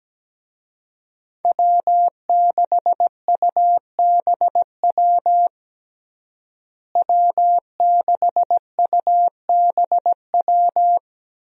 Morse code, heard as a keyed tone, about 17 wpm, 700 Hz. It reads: W6UBW W6UBW
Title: USA Call Signs - 5 Characters - Repeated 3x - 17wpm